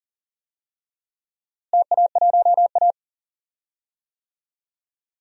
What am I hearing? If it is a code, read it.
Morse code, heard as a keyed tone, about 40 wpm, 700 Hz. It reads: TA1A
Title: DXCC-100 Call Signs - 4 Characters - Repeated 3x - 40wpm